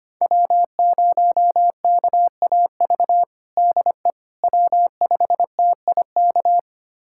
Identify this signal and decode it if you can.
Morse code, heard as a keyed tone, about 25 wpm, 700 Hz. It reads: W0KAV DE W5TIK
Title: USA Callsign DE Callsign - 5 Characters - 25wpm